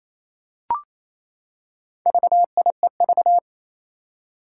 Morse code, heard as a keyed tone, about 29 wpm, 700 Hz. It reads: VIEV
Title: Four Challenging Letter-Number Combinations 28wpm